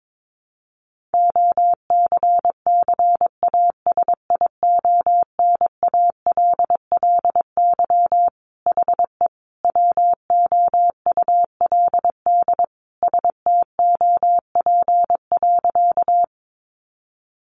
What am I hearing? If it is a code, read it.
Morse code, heard as a keyed tone, about 22 wpm, 700 Hz. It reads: OCCASIONALLY HE WOULD STOP.
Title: Sentences from Top 1300 Words 22wpm